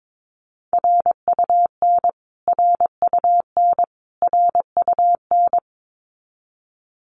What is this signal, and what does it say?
Morse code, heard as a keyed tone, about 22 wpm, 700 Hz. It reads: RUN RUN RUN
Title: Top 300 Words Repeated 22wpm